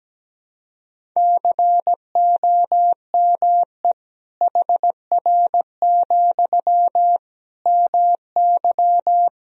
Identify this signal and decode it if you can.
Morse code, heard as a keyed tone, about 17 wpm, 700 Hz. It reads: COME HR, MY
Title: Sentences from Top 300 Words and Abbreviations 17wpm